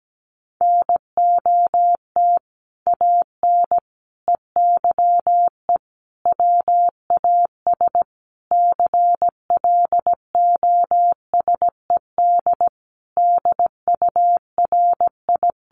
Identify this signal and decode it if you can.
Morse code, heard as a keyed tone, about 17 wpm, 700 Hz. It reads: NOT AN EYE WAS CLOSED DURI
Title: Sentences from Top 1400 Words 17wpm